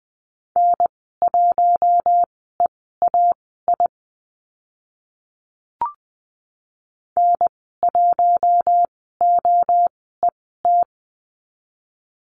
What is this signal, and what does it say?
Morse code, heard as a keyed tone, about 20 wpm, spaced as for 15 wpm, 700 Hz. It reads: N1EAI N1OET
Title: 042 Lesson 8g - Number 1 - Call Signs 20wpm